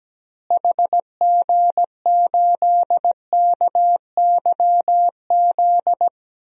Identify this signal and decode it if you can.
Morse code, heard as a keyed tone, about 17 wpm, 700 Hz. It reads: HG8KYZ
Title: DXCC-100 Call Signs - 6 Characters - Repeated 2x - 17wpm